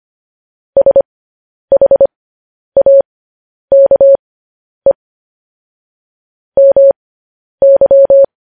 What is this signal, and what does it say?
Morse code, heard as a keyed tone, about 25 wpm, spaced as for 10 wpm, 550 Hz. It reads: SHAKE MY
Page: news headlines in morse code at 10 wpm